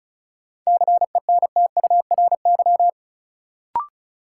Morse code, heard as a keyed tone, about 36 wpm, 700 Hz. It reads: CENTURY
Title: Sets of 5 Words - New Word Drill - 35wpm